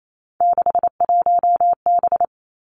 Morse code, heard as a keyed tone, about 28 wpm, 700 Hz. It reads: 61B
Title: Three Challenging Letter-Number Combinations 28wpm